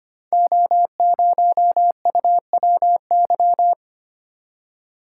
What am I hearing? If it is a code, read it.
Morse code, heard as a keyed tone, about 25 wpm, 700 Hz. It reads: O0UWY
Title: Code Groups - 5 Letter:Numbers - 25wpm